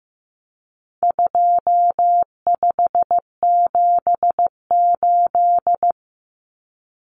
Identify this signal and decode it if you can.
Morse code, heard as a keyed tone, about 15 wpm, 700 Hz. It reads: 2578
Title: Common Cut Numbers - 4-Digit 15wpm